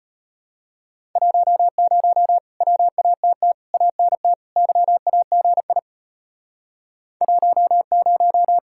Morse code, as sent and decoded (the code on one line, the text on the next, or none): .---- ----- / .-- .- - - / .- -. - / -.-- .- --. .. / .---- -----
10 WATT ANT YAGI 10